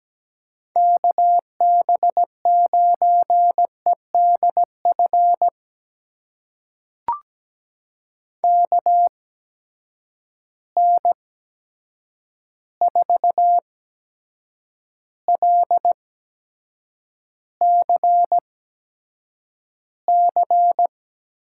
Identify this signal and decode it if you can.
Morse code, heard as a keyed tone, about 17 wpm, spaced as for 7 wpm, 700 Hz. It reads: KB9EDF KN4LCC
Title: USA Call Signs - 6 Characters - 8x Character Spacing - 17wpm